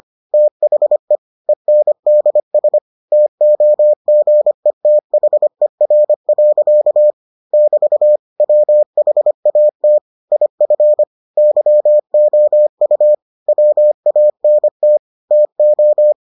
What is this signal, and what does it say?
Morse code, heard as a keyed tone, about 25 wpm, 600 Hz. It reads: THE ENDS TOGETHER. <BT> WHAT IF YOU WANT TO